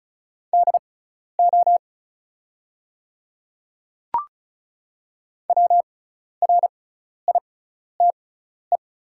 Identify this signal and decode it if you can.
Morse code, heard as a keyed tone, about 34 wpm, spaced as for 12 wpm, 700 Hz. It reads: DO WRITE
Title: 6x Character Spacing - Top 100 Words - 35wpm